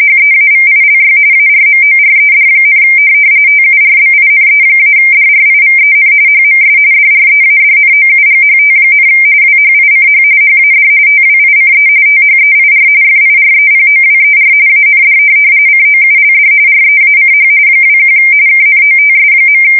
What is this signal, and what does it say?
RTTY, about 45.45 baud. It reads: HEAVILY ARMORED, TURBINE-DRIVEN HMS DREADNOUGHT, WHICH HAD TWICE THEIR FIREPOWER AND COULD REACH 21 KNOTS.
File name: fsk170 / USS-New-Jersey